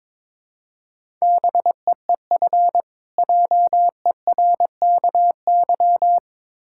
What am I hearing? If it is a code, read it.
Morse code, heard as a keyed tone, about 21 wpm, 690 Hz. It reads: BEEF JERKY
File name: Difficult Words with Hints - 22wpm